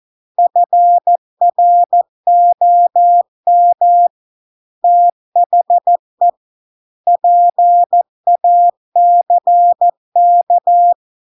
Morse code, heard as a keyed tone, about 14 wpm, 700 Hz. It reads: FROM THE PACK